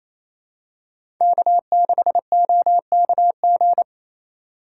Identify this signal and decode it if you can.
Morse code, heard as a keyed tone, about 28 wpm, 700 Hz. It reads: K6OKG
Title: USA Call Signs - 6x Character Spacing - 28wpm